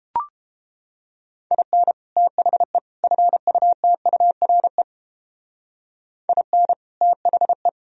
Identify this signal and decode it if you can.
Morse code, heard as a keyed tone, about 31 wpm, 700 Hz. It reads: IN THE FUTURE IN THE